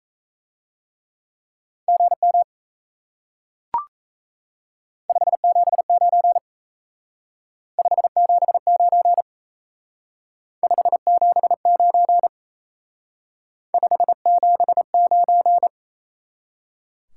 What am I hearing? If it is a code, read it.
Morse code, heard as a keyed tone, about 36 wpm, 700 Hz. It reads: GM 579 579 579 579